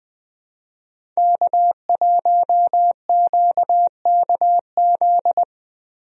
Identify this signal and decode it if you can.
Morse code, heard as a keyed tone, about 20 wpm, 700 Hz. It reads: K1QKZ
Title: USA Call Signs - 5 Characters - Repeated 2x - 20wpm